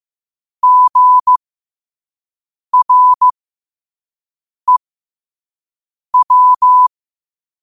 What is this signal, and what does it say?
Morse code, heard as a keyed tone, about 15 wpm, spaced as for 5 wpm, 1000 Hz. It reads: GREW
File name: qotdfiles / 5wpm